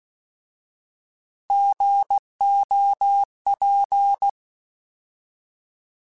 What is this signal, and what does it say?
Morse code, heard as a keyed tone, about 16 wpm, 800 Hz. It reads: GOP